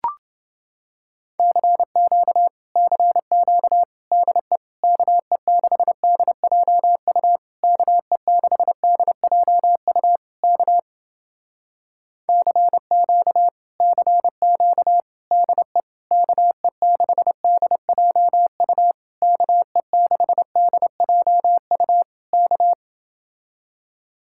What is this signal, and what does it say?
Morse code, heard as a keyed tone, about 30 wpm, 700 Hz. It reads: CQ CQ DE KE6DJU KE6DJU K CQ CQ DE KE6DJU KE6DJU K